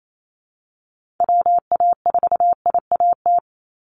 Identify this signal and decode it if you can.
Morse code, heard as a keyed tone, about 28 wpm, 700 Hz. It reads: WA4IAT
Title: USA Call Signs 28wpm